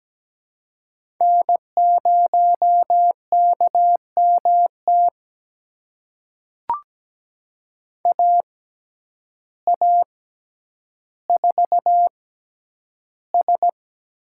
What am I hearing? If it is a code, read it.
Morse code, heard as a keyed tone, about 17 wpm, spaced as for 9 wpm, 700 Hz. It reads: N0KMT AA4S
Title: USA Call Signs - 5 Characters - 6x Character Spacing - 17wpm